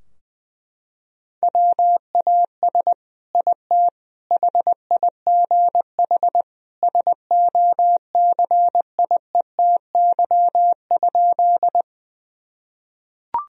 Morse code, heard as a keyed tone, about 20 wpm, 700 Hz. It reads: WAS IT HIGH SOCIETY?
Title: Sentences from Top 1000 Words - Review Part 4 - 20wpm